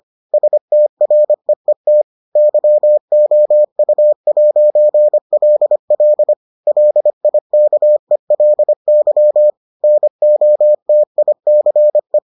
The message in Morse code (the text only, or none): STREET YOU'LL LIKELY NOTICE